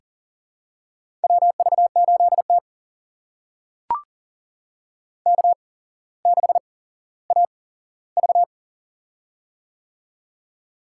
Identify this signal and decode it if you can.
Morse code, heard as a keyed tone, about 42 wpm, spaced as for 17 wpm, 700 Hz. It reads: WV8T K6AV
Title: USA Call Signs - 4 Characters - 8x Character Spacing - 40wpm